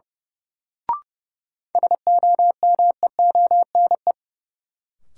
Morse code, heard as a keyed tone, about 30 wpm, 700 Hz. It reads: SOMEONE
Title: Sentences from Top 600 Words - New Word Drill - 30wpm